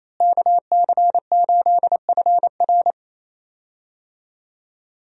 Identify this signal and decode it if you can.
Morse code, heard as a keyed tone, about 28 wpm, 700 Hz. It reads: KC8FR
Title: USA Call Signs - 5 Characters - Repeated 2x - 28wpm